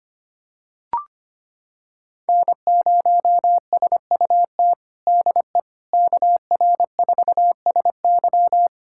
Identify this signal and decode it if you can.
Morse code, heard as a keyed tone, about 25 wpm, 700 Hz. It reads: N0SUT DE KR4SY
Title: USA Callsign DE Callsign - 5 Characters - 25wpm